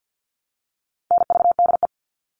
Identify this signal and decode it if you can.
Morse code, heard as a keyed tone, about 50 wpm, 700 Hz. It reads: NVDE
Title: All Cut Numbers - 4-Digit 50wpm